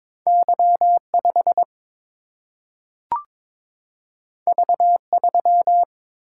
Morse code, heard as a keyed tone, about 22 wpm, 700 Hz. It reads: Y 5 V 3
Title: Single-Letter Single-Number 22wpm